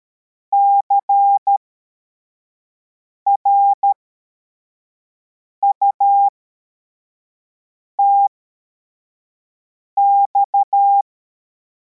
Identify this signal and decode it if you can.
Morse code, heard as a keyed tone, about 13 wpm, spaced as for 4 wpm, 800 Hz. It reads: CRUTX